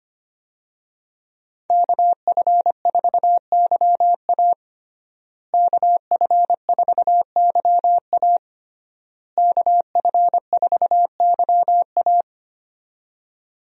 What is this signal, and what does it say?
Morse code, heard as a keyed tone, about 25 wpm, 700 Hz. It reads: KF4YA KF4YA KF4YA
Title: USA Call Signs - 5 Characters - Repeated 3x - 25wpm